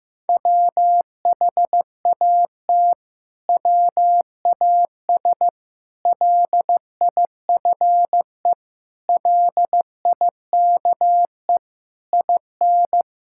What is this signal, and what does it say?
Morse code, heard as a keyed tone, about 15 wpm, 700 Hz. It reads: WHAT WAS LIFE LIKE IN
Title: Sentences from Top 1000 Words - Review Part 5 - 15wpm